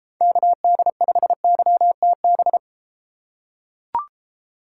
Morse code, heard as a keyed tone, about 32 wpm, 700 Hz. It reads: KD5YTB